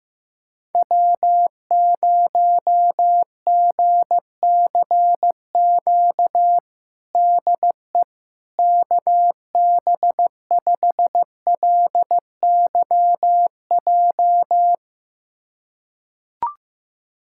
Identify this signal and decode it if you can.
Morse code, heard as a keyed tone, about 15 wpm, 700 Hz. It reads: W0GCQ DE KB5LYJ
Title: USA Callsign DE Callsign 15wpm